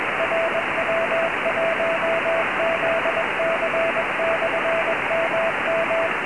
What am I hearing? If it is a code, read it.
Morse code, heard as a keyed tone, about 21 wpm, 660 Hz, with radio noise. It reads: RW1ZC/MM
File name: ei5fk / rw1zc ik18pq cq